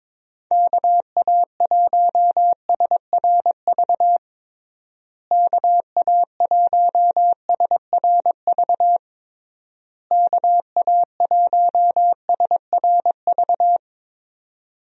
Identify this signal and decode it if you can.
Morse code, heard as a keyed tone, about 22 wpm, 700 Hz. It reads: KA1SRV KA1SRV KA1SRV
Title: USA Call Signs - 6 Characters - Repeated 3x - 22wpm